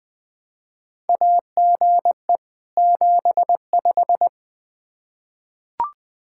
Morse code, heard as a keyed tone, about 20 wpm, 700 Hz. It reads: AGE 75
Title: K1USN SST QSO Practice 20wpm